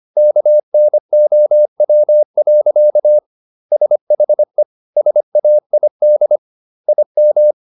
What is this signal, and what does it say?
Morse code, heard as a keyed tone, about 25 wpm, 600 Hz. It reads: KNOW. SHE SAID IM